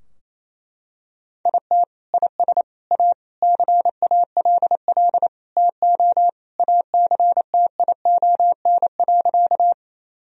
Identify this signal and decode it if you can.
Morse code, heard as a keyed tone, about 28 wpm, 700 Hz. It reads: IT IS A CALL TO ACTION.
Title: Sentences from Top 1000 Words - Review Part 4 - 28wpm